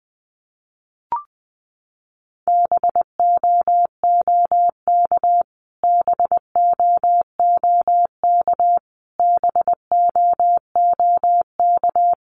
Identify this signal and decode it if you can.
Morse code, heard as a keyed tone, about 20 wpm, 700 Hz. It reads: BOOK BOOK BOOK
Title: Top 500 Words Repeated 20wpm